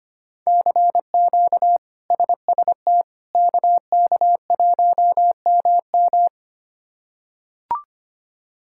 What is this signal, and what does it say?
Morse code, heard as a keyed tone, about 25 wpm, 700 Hz. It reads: CQ SST KK1MM
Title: K1USN SST QSO Practice - 25wpm